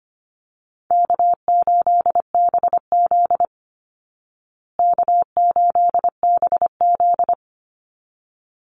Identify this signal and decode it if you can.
Morse code, heard as a keyed tone, about 25 wpm, 700 Hz. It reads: K8BZ K8BZ
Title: Easy Call Signs Repeated 2x - 25wpm